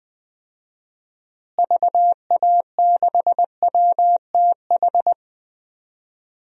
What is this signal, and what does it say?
Morse code, heard as a keyed tone, about 20 wpm, 700 Hz. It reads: VA6WTH
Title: DXCC-100 Call Signs - 6 Characters - Repeated 2x - 20wpm